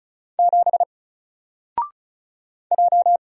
Morse code, heard as a keyed tone, about 36 wpm, 700 Hz. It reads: 7 J
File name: Single-Letter-Number-35wpm